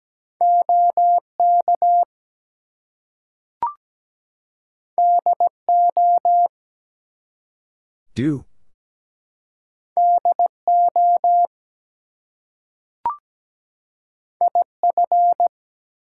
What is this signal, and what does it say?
Morse code, heard as a keyed tone, about 17 wpm, 700 Hz. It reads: OK DO DO IF
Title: Two Letter Words - 17wpm